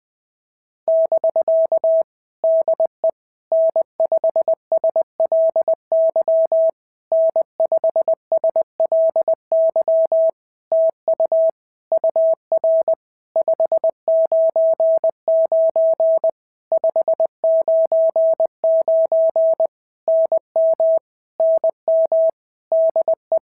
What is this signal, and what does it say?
Morse code, heard as a keyed tone, about 20 wpm, 650 Hz. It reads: <BK> DE N5SLY N5SLY TU UR 599 599 NM NM DE
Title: POTA QSO Practice - 20wpm